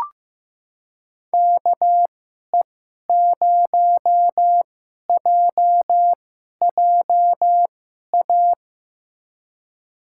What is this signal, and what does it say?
Morse code, heard as a keyed tone, about 15 wpm, spaced as for 11 wpm, 700 Hz. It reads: KE0JJA
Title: USA Call Signs - 2x Character Spacing - 15wpm